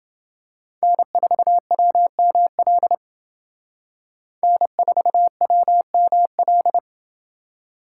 Morse code, heard as a keyed tone, about 29 wpm, 700 Hz. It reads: N4WML N4WML